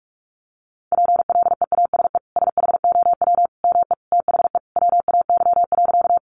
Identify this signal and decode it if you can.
Morse code, heard as a keyed tone, about 44 wpm, 700 Hz. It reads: PLEASE SHOW ME THE WAY.
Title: Sentences from Top 1500 Words 45wpm